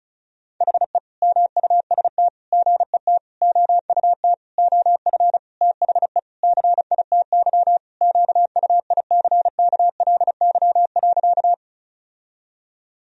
Morse code, heard as a keyed tone, about 35 wpm, 700 Hz. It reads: HE MUST GET OUT OF THE CITY QUICKLY.
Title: Sentences from Top 1000 Words - Review Part 1 - 35wpm